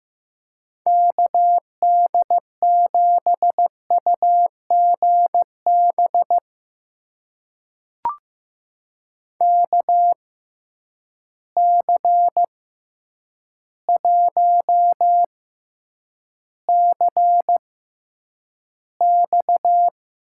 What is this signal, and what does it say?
Morse code, heard as a keyed tone, about 15 wpm, spaced as for 8 wpm, 700 Hz. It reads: KD7UGB KC1CX
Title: USA Call Signs - 6 Characters - 6x Character Spacing - 15wpm